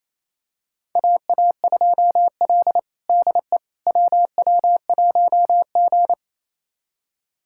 Morse code, heard as a keyed tone, about 27 wpm, 700 Hz. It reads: AA2L DE WW1G
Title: USA Callsign DE Callsign - 4 Characters - 28wpm